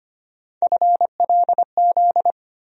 Morse code, heard as a keyed tone, about 25 wpm, 700 Hz. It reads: FLZ
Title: USA Call Sign Postfixes - 3 Letters 25wpm